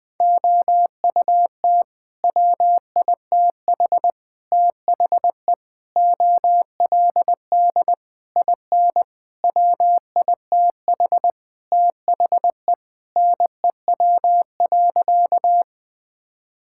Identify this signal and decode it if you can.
Morse code, heard as a keyed tone, about 21 wpm, 700 Hz. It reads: OUT WITH THE OLD IN WITH THE NEW.
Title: Sentences with 8-Word Phrases - Spoken Beginning - 20wpm